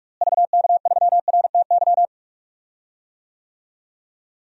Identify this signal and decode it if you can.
Morse code, heard as a keyed tone, about 43 wpm, 700 Hz. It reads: VK3RTY
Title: DXCC-100 Call Signs - 6 Characters - Repeated 3x - 45wpm